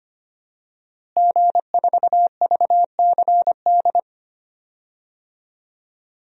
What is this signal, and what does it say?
Morse code, heard as a keyed tone, about 25 wpm, 700 Hz. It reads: G4VCD
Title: DXCC-100 Call Signs - 5 Characters - Repeated 2x - 25wpm